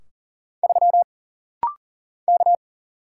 Morse code, heard as a keyed tone, about 40 wpm, 700 Hz. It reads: 3 K
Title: Single Letter-Number Alt - Rapid-Fire 40wpm